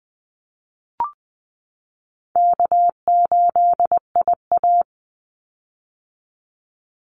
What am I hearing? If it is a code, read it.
Morse code, heard as a keyed tone, about 20 wpm, 700 Hz. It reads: K8IA
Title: Call Signs - 20wpm